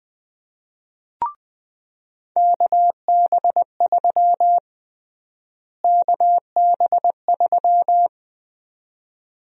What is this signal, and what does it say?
Morse code, heard as a keyed tone, about 21 wpm, 700 Hz. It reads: KB3 KB3
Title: USA Call Sign Prefixes - Repeated 2x - 20wpm